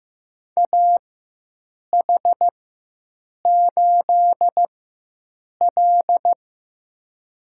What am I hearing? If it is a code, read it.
Morse code, heard as a keyed tone, about 16 wpm, spaced as for 7 wpm, 700 Hz. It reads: AH8L
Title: USA Call Signs - 4 Characters - 4x Character Spacing - 15wpm